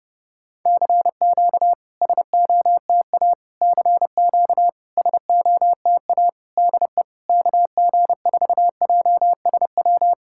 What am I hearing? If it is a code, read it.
Morse code, heard as a keyed tone, about 30 wpm, 700 Hz. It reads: CQ SOTA CQ SOTA DE KG4JSW
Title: SOTA QSO Practice 30wpm